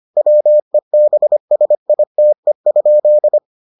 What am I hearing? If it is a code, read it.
Morse code, heard as a keyed tone, about 25 wpm, 600 Hz. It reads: WEBSITE?